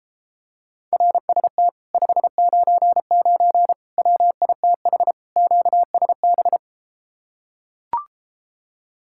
Morse code, heard as a keyed tone, about 33 wpm, 700 Hz. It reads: RST 599 WITH QSB